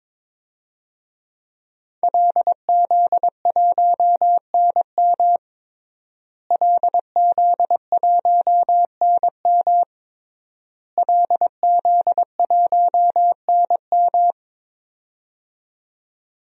Morse code, heard as a keyed tone, about 22 wpm, 700 Hz. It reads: LZ1NM LZ1NM LZ1NM
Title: DXCC-100 Call Signs - 5 Characters - Repeated 3x - 22wpm